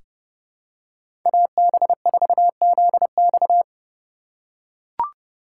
Morse code, heard as a keyed tone, about 30 wpm, 700 Hz. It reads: AB4ZX
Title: USA Call Signs - 5 Characters - 3x Character Spacing - 30wpm